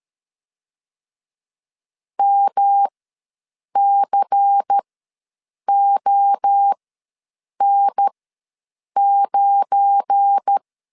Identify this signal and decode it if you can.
Morse code, heard as a keyed tone, about 13 wpm, spaced as for 7 wpm, 800 Hz. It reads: MCON9
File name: LEZIONI / 8 YJ45 / 8 YJ ws 12wpm 3342